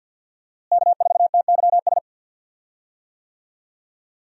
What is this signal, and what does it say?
Morse code, heard as a keyed tone, about 46 wpm, 700 Hz. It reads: K4TYS
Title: USA Call Signs - 5 Characters - Repeated 2x - 50wpm